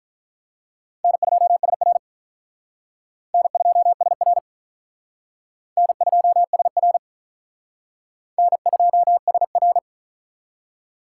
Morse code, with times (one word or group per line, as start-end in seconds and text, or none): 1.0-2.0 N2SR
3.3-4.4 N2SR
5.8-7.0 N2SR
8.4-9.8 N2SR